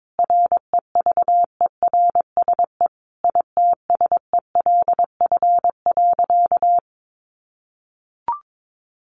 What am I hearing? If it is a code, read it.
Morse code, heard as a keyed tone, about 22 wpm, 700 Hz. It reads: REVERSE ITSELF.
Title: A Sound Of Thunder 22wpm